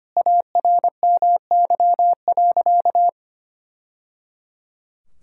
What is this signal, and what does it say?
Morse code, heard as a keyed tone, about 25 wpm, 700 Hz. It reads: ARMY.
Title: Sentences from Top 1000 Words - Review Part 5 - 25wpm